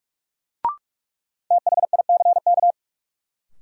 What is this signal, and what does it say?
Morse code, heard as a keyed tone, about 46 wpm, 700 Hz. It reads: THICK
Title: Sentences from Top 1600 Words - New Word Drill Alt - 45wpm